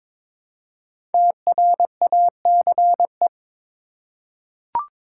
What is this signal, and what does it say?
Morse code, heard as a keyed tone, about 22 wpm, 700 Hz. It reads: TRACE
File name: 5 Letter Words - First 3 Letters Given - 22wpm